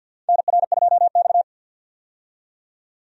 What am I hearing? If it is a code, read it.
Morse code, heard as a keyed tone, about 47 wpm, 700 Hz. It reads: NR2X
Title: USA Call Signs - 4 Characters - 50wpm